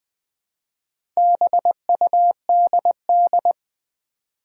Morse code, heard as a keyed tone, about 20 wpm, 700 Hz. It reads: BUDD
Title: Build A Word - 1 Letter At A Time - 5-Letter Words - 20wpm